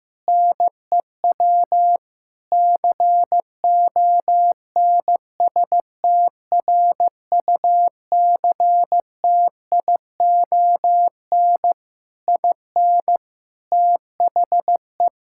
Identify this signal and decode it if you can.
Morse code, heard as a keyed tone, about 15 wpm, 700 Hz. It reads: NEW CONSTRUCTION IN THE
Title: Sentences from Top 1600 Words 15wpm